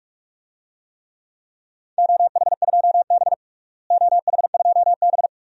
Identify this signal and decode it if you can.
Morse code, heard as a keyed tone, about 46 wpm, 700 Hz. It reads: OH2B OH2B
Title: DXCC-100 Call Signs - 4 Characters - Repeated 2x - 45wpm